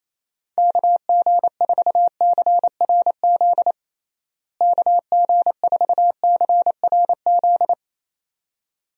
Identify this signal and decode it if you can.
Morse code, heard as a keyed tone, about 28 wpm, 700 Hz. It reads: KG4CRZ KG4CRZ
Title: USA Call Signs - 6 Characters - Repeated 2x - 28wpm